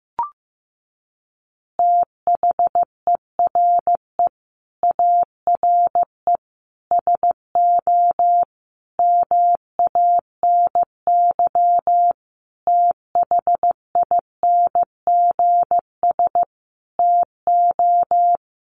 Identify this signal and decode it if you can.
Morse code, heard as a keyed tone, about 15 wpm, 700 Hz. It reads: THERE ARE SO MANY THINGS TO
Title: Additional Easy Phrases 15wpm